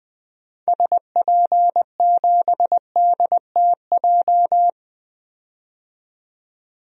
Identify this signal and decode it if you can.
Morse code, heard as a keyed tone, about 21 wpm, 700 Hz. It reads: SP7DTJ
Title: DXCC-100 Call Signs - 6 Characters - 20wpm